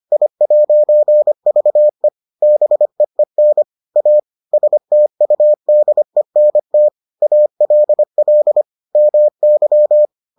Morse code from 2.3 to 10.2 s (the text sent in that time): BEEN A STUDENT ALL MY